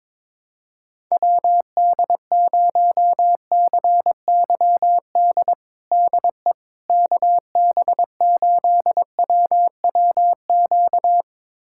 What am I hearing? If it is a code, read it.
Morse code, heard as a keyed tone, about 22 wpm, 700 Hz. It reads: WD0CYD DE KB8WWQ
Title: USA Callsign DE Callsign - 6 Characters - 22wpm